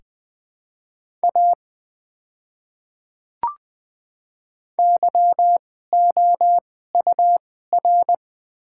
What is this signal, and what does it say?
Morse code, heard as a keyed tone, about 20 wpm, spaced as for 14 wpm, 700 Hz. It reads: A YOUR
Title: 2x Character Spacing - Top 100 Words - 20wpm